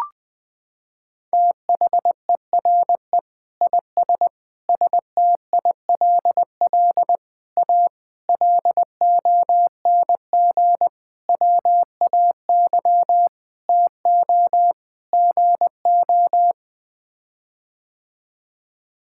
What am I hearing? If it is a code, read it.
Morse code, heard as a keyed tone, about 20 wpm, 700 Hz. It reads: THERE IS STILL A LONG WAY TO GO
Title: Sets of 8 Words - 20wpm